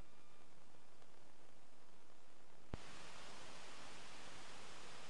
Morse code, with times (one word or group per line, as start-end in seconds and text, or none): none